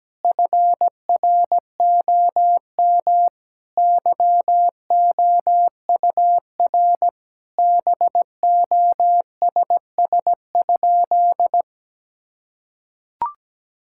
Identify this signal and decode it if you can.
Morse code, heard as a keyed tone, about 17 wpm, 700 Hz. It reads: FROM YOUR BOSS?
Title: Sentences from Top 1000 Words - Review Part 6 - 17wpm